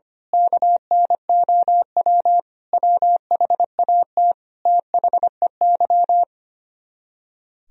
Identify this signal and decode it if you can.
Morse code, heard as a keyed tone, about 24 wpm, 700 Hz. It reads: KNOW WHAT THEY
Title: Speed-Racing - Sets of 5 Words 25wpm